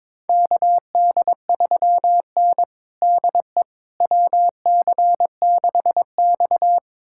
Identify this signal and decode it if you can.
Morse code, heard as a keyed tone, about 22 wpm, 700 Hz. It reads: KD3N DE WC6X
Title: USA Callsign DE Callsign - 4 Characters - 22wpm